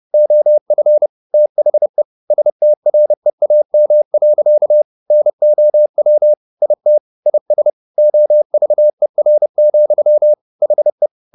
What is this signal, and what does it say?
Morse code, heard as a keyed tone, about 30 wpm, 600 Hz. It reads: OF THE STREAM. NOW IT IS OVER, HE